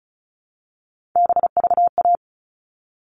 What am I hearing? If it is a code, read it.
Morse code, heard as a keyed tone, about 34 wpm, 700 Hz. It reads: BVA
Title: All Cut Numbers - 3-Digit 35wpm